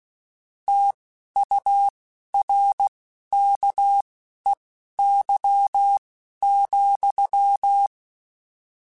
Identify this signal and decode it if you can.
Morse code, heard as a keyed tone, about 16 wpm, spaced as for 11 wpm, 800 Hz. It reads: TURKEY,
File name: Misc / Misc-mp3s / Word Pairs @10